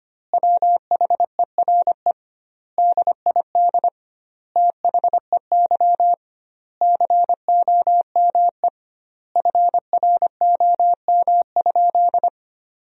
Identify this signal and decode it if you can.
Morse code, heard as a keyed tone, about 25 wpm, 700 Hz. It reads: WHERE DID THEY COME FROM?
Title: 2x Word-Spacing - Sentences from Top 200 Words - 25wpm